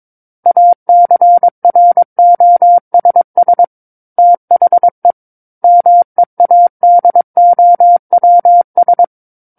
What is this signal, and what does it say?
Morse code, heard as a keyed tone, about 22 wpm, 700 Hz. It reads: ACROSS THE MEADOWS
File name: Book One / 13-Chapter 13 - 22 WPM